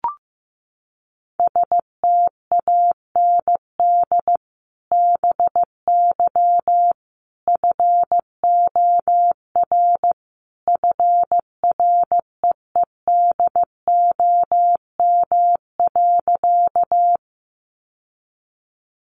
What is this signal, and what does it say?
Morse code, heard as a keyed tone, about 15 wpm, 700 Hz. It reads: STAND BY FOR FREEDOM.
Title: Sentences from Top 800 Words 15wpm